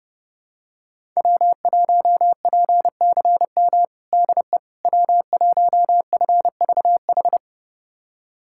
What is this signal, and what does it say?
Morse code, heard as a keyed tone, about 30 wpm, 700 Hz. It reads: W1PCM DE W1FVH
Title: USA Callsign DE Callsign - 5 Characters - 30wpm